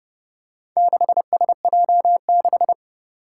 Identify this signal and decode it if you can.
Morse code, heard as a keyed tone, about 30 wpm, 700 Hz. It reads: 6SJ6
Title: Four Challenging Letter-Number Combinations 30wpm